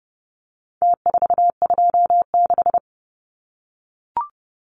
Morse code, heard as a keyed tone, about 30 wpm, 700 Hz. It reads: T426
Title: Numbers - Contest Style - 4-Digit 30wpm